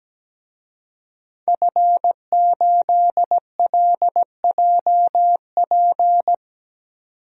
Morse code, heard as a keyed tone, about 17 wpm, 700 Hz. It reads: F8LJP
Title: DXCC-100 Call Signs - 5 Characters - 17wpm